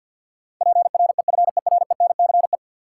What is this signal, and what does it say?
Morse code, heard as a keyed tone, about 52 wpm, 700 Hz. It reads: PREFERENCE
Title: Sentences from Top 1500 Words - New Word Drill Alt - 50wpm